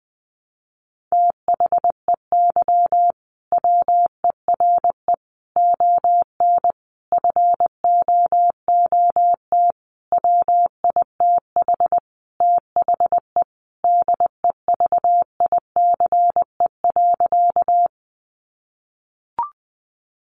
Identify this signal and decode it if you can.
Morse code, heard as a keyed tone, about 20 wpm, 700 Hz. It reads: THEY WERE ON FOOT WITH THE DEVICE.
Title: Sentences from Top 1500 Words 20wpm